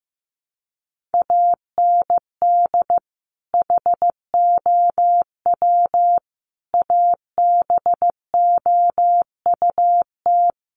Morse code, heard as a keyed tone, about 15 wpm, 700 Hz. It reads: AND HOW ABOUT
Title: Sentences from Top 1200 Words 15wpm